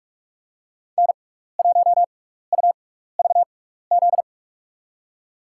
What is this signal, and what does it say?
Morse code, heard as a keyed tone, about 47 wpm, spaced as for 16 wpm, 700 Hz. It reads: N1UVZ
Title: USA Call Signs - 5 Characters - 6x Character Spacing - 45wpm